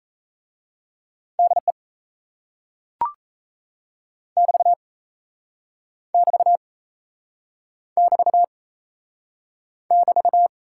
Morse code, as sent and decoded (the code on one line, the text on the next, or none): -.. / -...- / -...- / -...- / -...-
DE <BT> <BT> <BT> <BT>